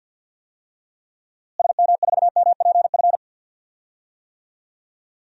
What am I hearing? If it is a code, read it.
Morse code, heard as a keyed tone, about 46 wpm, 700 Hz. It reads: SM4MPF
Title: DXCC-100 Call Signs - 6 Characters - Repeated 2x - 50wpm